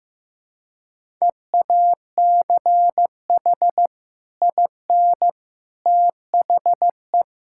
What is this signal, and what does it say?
Morse code, heard as a keyed tone, about 16 wpm, 700 Hz. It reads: EACH IN THE
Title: Sentences from Top 1600 Words 15wpm